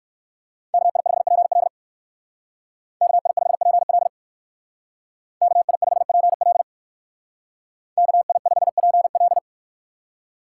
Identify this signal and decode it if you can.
Morse code, heard as a keyed tone, about 59 wpm, 700 Hz. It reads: KI5PL KI5PL KI5PL KI5PL